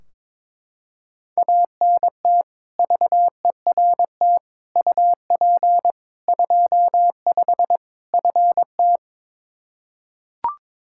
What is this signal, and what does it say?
Morse code, heard as a keyed tone, about 22 wpm, 700 Hz. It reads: ANT VERT UP 25 FT
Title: K1USN SST QSO Practice 22wpm